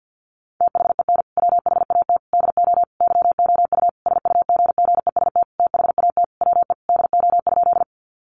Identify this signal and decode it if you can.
Morse code, heard as a keyed tone, about 50 wpm, 700 Hz. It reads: THEN WHAT DO YOU SUGGEST THAT WE DO?
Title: Sentences from Top 1300 Words 50wpm